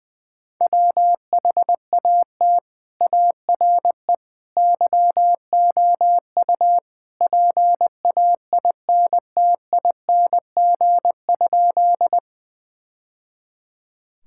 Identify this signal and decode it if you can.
Morse code, heard as a keyed tone, about 20 wpm, 700 Hz. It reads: WHAT ARE YOU PAINTING?
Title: Sentences from Top 1000 Words - Review Part 6 - 20wpm